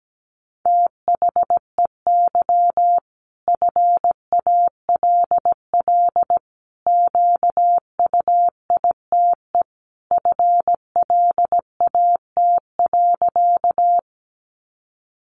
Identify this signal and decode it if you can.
Morse code, heard as a keyed tone, about 17 wpm, 700 Hz. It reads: THEY FALL QUITE FLAT.
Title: Sentences from Top 1000 Words 17wpm